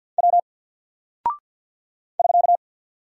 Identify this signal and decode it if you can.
Morse code, heard as a keyed tone, about 48 wpm, 700 Hz. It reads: W <SK>
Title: Single Character (Letter:Number:Punctuation:Prosign) 50wpm